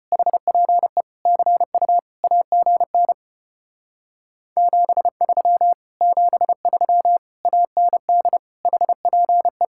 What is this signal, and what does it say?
Morse code, heard as a keyed tone, about 33 wpm, 700 Hz. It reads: HPE CU AGN 73 73 AND HPE